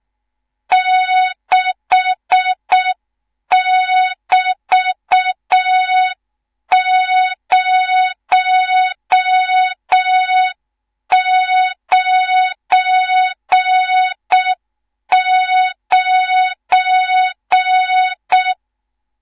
Morse code, heard as a keyed tone, about 6 wpm, 3700 Hz. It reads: 6<BT>099
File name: Morze-HA2MN / hang / 18 IRASJELEK2